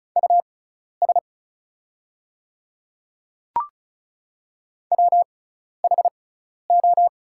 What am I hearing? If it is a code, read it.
Morse code, heard as a keyed tone, about 36 wpm, spaced as for 12 wpm, 690 Hz. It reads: US WHO
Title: 6x Character Spacing - Top 300 Words - 35wpm